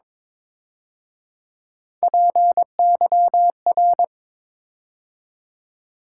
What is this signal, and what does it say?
Morse code, heard as a keyed tone, about 26 wpm, 700 Hz. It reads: PYR
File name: Speed-Racing - Three Letter-Number Combinations - 22wpm